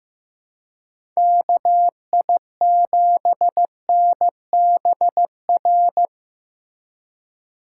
Morse code, heard as a keyed tone, about 15 wpm, 700 Hz. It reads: KI7NBR
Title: USA Call Signs - 6 Characters - 4x Character Spacing - 15wpm